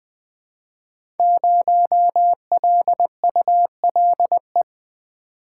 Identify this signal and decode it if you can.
Morse code, heard as a keyed tone, about 20 wpm, 700 Hz. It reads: 0LULE
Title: Code Groups - 5 Letter:Numbers - 20wpm